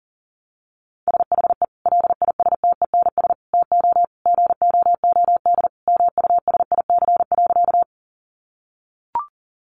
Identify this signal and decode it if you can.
Morse code, heard as a keyed tone, about 40 wpm, 700 Hz. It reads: SHE LISTENS TO GOOD MUSIC.
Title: Easy Phrases 40wpm